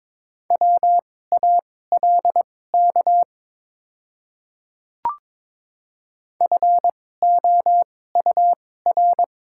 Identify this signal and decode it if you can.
Morse code, heard as a keyed tone, about 22 wpm, spaced as for 16 wpm, 690 Hz. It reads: WALK FOUR
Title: 2x Character Spacing - Top 300 Words - 22wpm